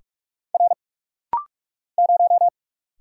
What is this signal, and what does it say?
Morse code, heard as a keyed tone, about 44 wpm, 690 Hz. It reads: R 0
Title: Single Letter-Number - Rapid-Fire 45wpm